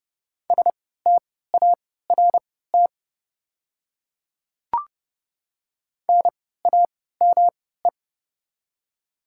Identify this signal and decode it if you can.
Morse code, heard as a keyed tone, about 30 wpm, spaced as for 17 wpm, 700 Hz. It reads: START NAME
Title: 3x Character Spacing - Top 300 Words - 30wpm